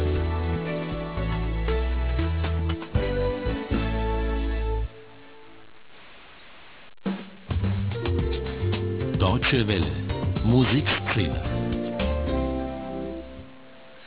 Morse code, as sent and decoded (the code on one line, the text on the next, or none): none